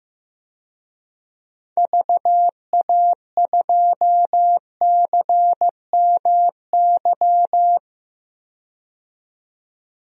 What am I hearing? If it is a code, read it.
Morse code, heard as a keyed tone, about 16 wpm, 700 Hz. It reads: VA2CMY
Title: DXCC-100 Call Signs - 6 Characters - 15wpm